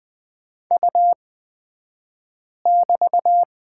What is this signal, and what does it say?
Morse code, heard as a keyed tone, about 20 wpm, spaced as for 5 wpm, 700 Hz. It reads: U<BT>
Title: 223 Lesson 36d - BT - Two Characters 20wpm